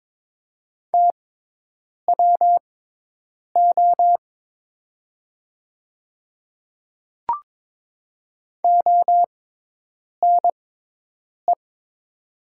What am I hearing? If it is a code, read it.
Morse code, heard as a keyed tone, about 23 wpm, spaced as for 8 wpm, 700 Hz. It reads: TWO ONE
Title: 6x Character Spacing - Top 100 Words - 22wpm